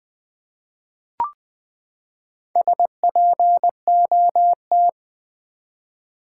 Morse code, heard as a keyed tone, about 20 wpm, 700 Hz. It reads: SPOT
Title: Sentences from Top 1100 Words - New Word Drill Alt v2 - 20wpm